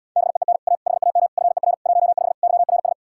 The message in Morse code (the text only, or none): BEEN A HAM FER 15 YRS